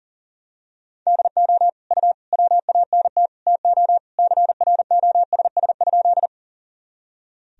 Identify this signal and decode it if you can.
Morse code, heard as a keyed tone, about 40 wpm, 700 Hz. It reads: DO U WANT TO CROSS?
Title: Sentences from Top 300 Words and Abbreviations 40wpm